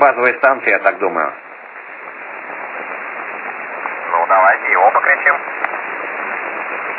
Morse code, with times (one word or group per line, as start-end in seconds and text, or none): none